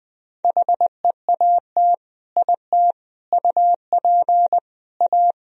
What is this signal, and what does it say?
Morse code, heard as a keyed tone, about 20 wpm, 700 Hz. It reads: HEAT IT UP A
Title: Sentences from Top 1000 Words - Review Part 3 - 20wpm